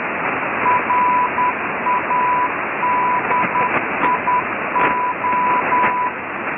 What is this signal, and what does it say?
Morse code, heard as a keyed tone, about 10 wpm, 990 Hz, with radio noise. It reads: RADIO